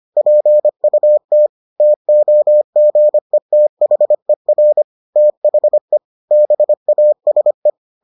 Morse code, heard as a keyed tone, about 25 wpm, 600 Hz. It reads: PUT TOGETHER THE BASE